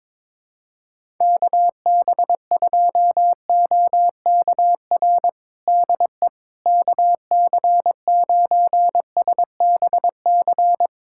Morse code, heard as a keyed tone, about 22 wpm, 700 Hz. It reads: KB2OKR DE KC9SBC
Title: USA Callsign DE Callsign - 6 Characters - 22wpm